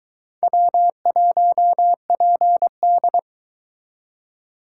Morse code, heard as a keyed tone, about 22 wpm, 700 Hz. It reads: W1PD